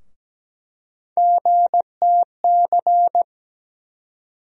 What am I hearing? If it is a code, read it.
Morse code, heard as a keyed tone, about 17 wpm, 700 Hz. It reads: GTC
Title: USA Call Sign Postfixes - 3 Letters 17wpm